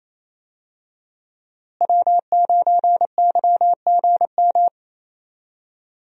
Morse code, heard as a keyed tone, about 28 wpm, 700 Hz. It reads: W9YGM
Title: USA Call Signs - 5 Characters - Repeated 3x - 28wpm